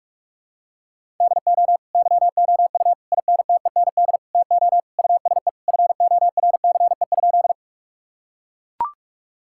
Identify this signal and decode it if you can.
Morse code, heard as a keyed tone, about 43 wpm, 700 Hz. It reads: DO YOU INTEND TO USE FORCE?
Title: Sentences from Top 1600 Words 45wpm